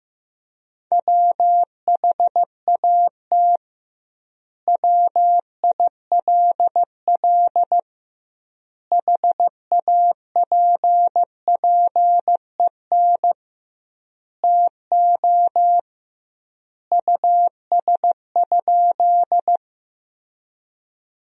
Morse code, heard as a keyed tone, about 15 wpm, 700 Hz. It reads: WHAT WILL HAPPEN TO US?
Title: 2x Word-Spacing - Sentences from Top 400 Words - 15wpm